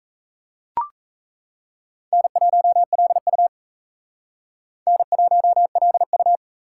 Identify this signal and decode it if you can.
Morse code, heard as a keyed tone, about 34 wpm, 700 Hz. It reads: N1LU N1LU
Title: Speed Racing - USA 1x2 Call Signs 28wpm